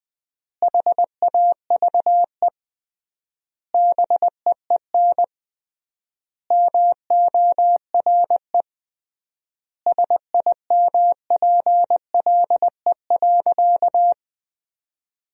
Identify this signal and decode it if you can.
Morse code, heard as a keyed tone, about 20 wpm, 700 Hz. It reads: HAVE BEEN MORE SIMPLE.
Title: 3x Word-Spacing - Sentences from Top 400 Words - 20wpm